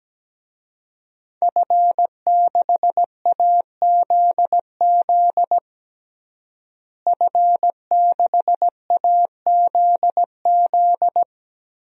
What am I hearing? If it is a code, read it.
Morse code, heard as a keyed tone, about 17 wpm, 700 Hz. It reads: F6AZZ F6AZZ
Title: DXCC-100 Call Signs - 5 Characters - Repeated 3x - 17wpm